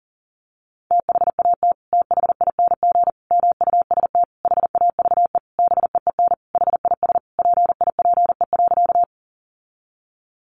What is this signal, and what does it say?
Morse code, heard as a keyed tone, about 40 wpm, 700 Hz. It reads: THAT THING MUST HAVE BEEN HIS PIPE.
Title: Sentences from Top 1500 Words 40wpm